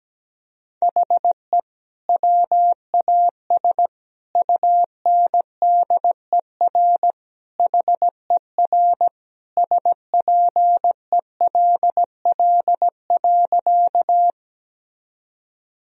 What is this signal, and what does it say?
Morse code, heard as a keyed tone, about 17 wpm, 700 Hz. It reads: HE WAS UNDER HER SPELL.
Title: Sentences from Top 1000 Words - Review Part 1 - 17wpm